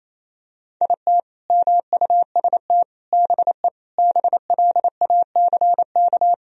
Morse code, heard as a keyed tone, about 28 wpm, 700 Hz. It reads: IT MUST BE BLACK
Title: Sentences from Top 1000 Words - Review Part 2 - 28wpm